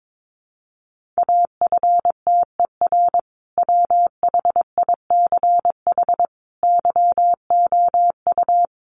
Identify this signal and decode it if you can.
Morse code, heard as a keyed tone, about 22 wpm, 700 Hz. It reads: AFTER WHICH YOU
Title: Sentences from Top 1400 Words 22wpm